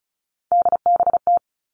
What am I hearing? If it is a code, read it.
Morse code, heard as a keyed tone, about 34 wpm, 700 Hz. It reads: DBT
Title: All Cut Numbers - 3-Digit 35wpm